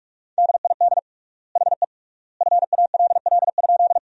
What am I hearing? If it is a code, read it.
Morse code, heard as a keyed tone, about 44 wpm, 700 Hz. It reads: DID HE FALL?
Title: 3x Word-Spacing - Sentences from Top 500 Words - 45wpm